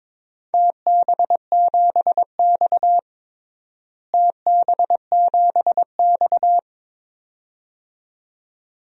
Morse code, heard as a keyed tone, about 22 wpm, 700 Hz. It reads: TB7X TB7X
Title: DXCC-100 Call Signs - 4 Characters - Repeated 2x - 22wpm